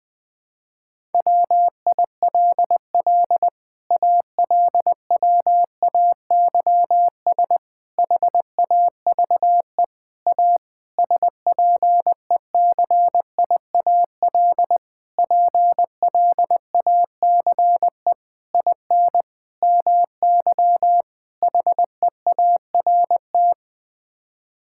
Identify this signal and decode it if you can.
Morse code, heard as a keyed tone, about 21 wpm, 700 Hz. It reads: WILL ALWAYS HAVE A SPECIAL PLACE IN MY HEART
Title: Sets of 9 Words - 20wpm